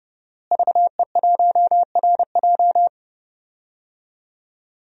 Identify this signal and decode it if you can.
Morse code, heard as a keyed tone, about 30 wpm, 700 Hz. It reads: VE1RJ
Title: DXCC-100 Call Signs - 5 Characters - Repeated 3x - 30wpm